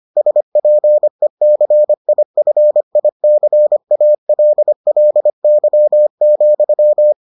Morse code, heard as a keyed tone, about 25 wpm, 600 Hz. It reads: SPECIFICALLY,